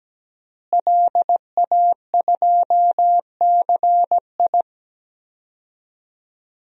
Morse code, heard as a keyed tone, about 17 wpm, 700 Hz. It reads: LA2CI
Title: DXCC-100 Call Signs - 5 Characters - Repeated 2x - 17wpm